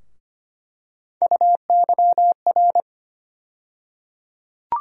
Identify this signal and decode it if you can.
Morse code, heard as a keyed tone, about 25 wpm, 700 Hz. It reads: UYR
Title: USA Call Sign Postfixes - 3 Letters 25wpm